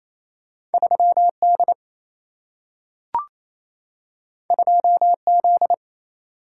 Morse code, heard as a keyed tone, about 28 wpm, 700 Hz. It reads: 3D 2Z
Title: Single-Letter Single-Number 28wpm